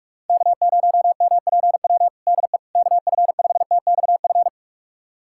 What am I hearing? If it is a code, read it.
Morse code, heard as a keyed tone, about 46 wpm, 700 Hz. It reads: K0MPW DE KF5TXF
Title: USA Callsign DE Callsign 45wpm